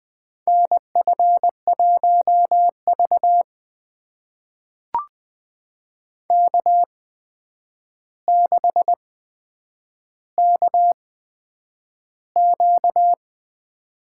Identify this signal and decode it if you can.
Morse code, heard as a keyed tone, about 20 wpm, spaced as for 9 wpm, 700 Hz. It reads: NF1V K6KQ
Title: USA Call Signs - 4 Characters - 8x Character Spacing - 20wpm